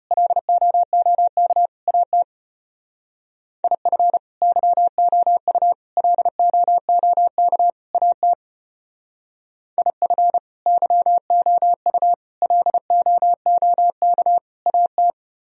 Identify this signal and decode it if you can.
Morse code, heard as a keyed tone, about 31 wpm, 700 Hz. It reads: LOOK AT IF YOU LOOK AT IF YOU LOOK AT
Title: Speed-Racing - Sets of 4 Words 25wpm